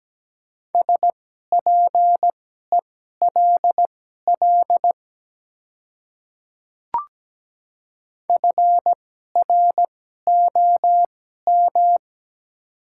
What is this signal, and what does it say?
Morse code, heard as a keyed tone, about 17 wpm, spaced as for 12 wpm, 700 Hz. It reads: SPELL FROM
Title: 2x Character Spacing - Top 300 Words - 17wpm